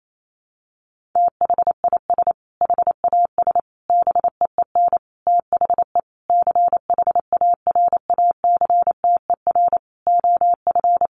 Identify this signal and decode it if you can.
Morse code, heard as a keyed tone, about 28 wpm, 700 Hz. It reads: THIS HAS BEEN THE CHARACTER OF
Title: Sentences from Top 1400 Words 28wpm